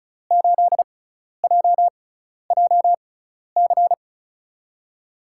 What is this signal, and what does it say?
Morse code, heard as a keyed tone, about 36 wpm, spaced as for 12 wpm, 700 Hz. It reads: 8JJC
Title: USA Call Signs - 5 Characters - 6x Character Spacing - 35wpm